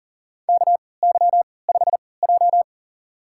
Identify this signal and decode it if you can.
Morse code, heard as a keyed tone, about 40 wpm, spaced as for 23 wpm, 700 Hz. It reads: KY5J
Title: USA Call Signs - 3x Character Spacing - 40wpm